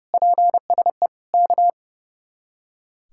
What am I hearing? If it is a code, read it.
Morse code, heard as a keyed tone, about 36 wpm, 700 Hz. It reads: PSE K
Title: Speed-Racing Common QSO Phrases Encore - 30wpm